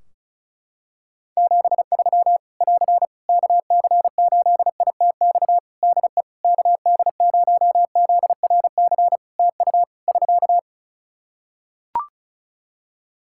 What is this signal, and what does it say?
Morse code, heard as a keyed tone, about 34 wpm, 700 Hz. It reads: 73 <AR> KC8ITX DE KD0ZRC TU <SK>
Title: Practice QSO - Exchange 4 - Ending - 35wpm